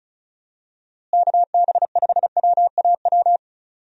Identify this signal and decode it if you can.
Morse code, heard as a keyed tone, about 36 wpm, 700 Hz. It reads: KB5WAW
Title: USA Call Signs - 6 Characters - Repeated 3x - 35wpm